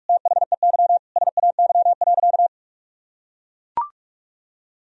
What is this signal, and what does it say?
Morse code, heard as a keyed tone, about 41 wpm, 700 Hz. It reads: THEY SAY.
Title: Sentences from Top 1000 Words - Review Part 1 - 45wpm